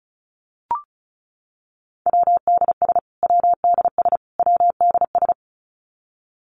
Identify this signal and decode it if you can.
Morse code, heard as a keyed tone, about 34 wpm, 700 Hz. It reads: WDS WDS WDS
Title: Abbreviations Repeated 3x - 35wpm